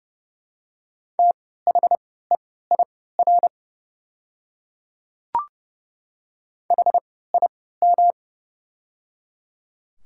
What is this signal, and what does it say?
Morse code, heard as a keyed tone, about 30 wpm, spaced as for 17 wpm, 690 Hz. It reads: THEIR HIM
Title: 3x Character Spacing - Top 500 Words - 30wpm